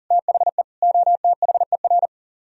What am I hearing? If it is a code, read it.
Morse code, heard as a keyed tone, about 49 wpm, 700 Hz. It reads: THE OTHER
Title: Speed-Racing - Sets of 3 Words 40wpm